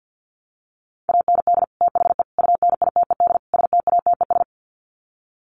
Morse code, heard as a keyed tone, about 49 wpm, 700 Hz. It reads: AND THE UNITED STATES
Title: Sets of 4 Words 50wpm